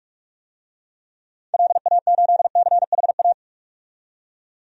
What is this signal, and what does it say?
Morse code, heard as a keyed tone, about 44 wpm, 690 Hz. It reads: LA8CHA
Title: DXCC-100 Call Signs - 6 Characters - 45wpm